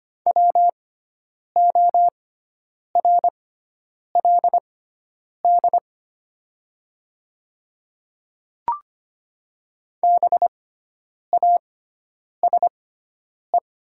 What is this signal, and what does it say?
Morse code, heard as a keyed tone, about 25 wpm, spaced as for 9 wpm, 700 Hz. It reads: WORLD BASE
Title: 6x Character Spacing - Top 300 Words - 25wpm